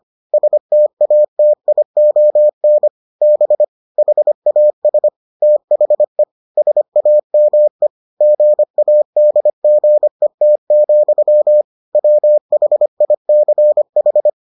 Morse code, heard as a keyed tone, about 25 wpm, 600 Hz. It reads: STATION B HAS THE SAME GADGET, WHICH